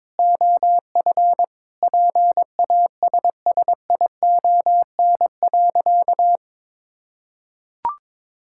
Morse code, heard as a keyed tone, about 22 wpm, 700 Hz. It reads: OF PASSION.
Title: Sentences from Top 1600 Words 22wpm